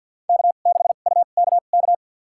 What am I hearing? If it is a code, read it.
Morse code, heard as a keyed tone, about 51 wpm, spaced as for 36 wpm, 700 Hz. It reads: K6UKK
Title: USA Call Signs - 5 Characters - 2x Character Spacing - 50wpm